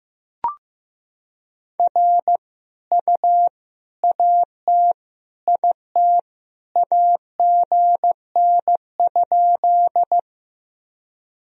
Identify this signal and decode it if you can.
Morse code, heard as a keyed tone, about 15 wpm, 700 Hz. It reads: R U AT IT AGN?
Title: Sentences from Top 200 Words and Abbreviations 15wpm